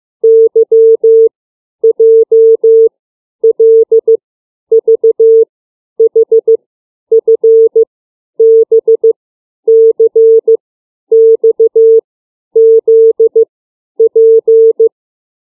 Morse code, heard as a keyed tone, about 15 wpm, 450 Hz. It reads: Y J L V H F B C X Z P
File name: CW / alphabet 6